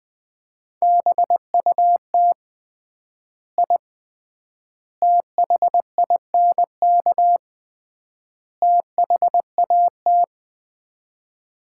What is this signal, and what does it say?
Morse code, heard as a keyed tone, about 21 wpm, 700 Hz. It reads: BUT I THINK THAT
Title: 3x Word-Spacing - Sets of 4 Words - 20wpm